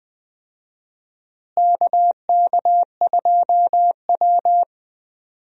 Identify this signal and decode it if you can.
Morse code, heard as a keyed tone, about 20 wpm, 700 Hz. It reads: KK2W
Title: USA Call Signs - 4 Characters - Repeated 2x - 20wpm